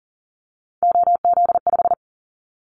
Morse code, heard as a keyed tone, about 38 wpm, 700 Hz. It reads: OZ5
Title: Callsign Prefixes for Top 50 DXCC - 40wpm